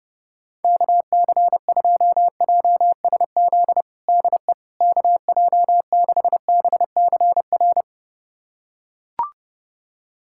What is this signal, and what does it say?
Morse code, heard as a keyed tone, about 30 wpm, 700 Hz. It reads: KC2JSZ DE KJ6BCR
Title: USA Callsign DE Callsign - 6 Characters - 30wpm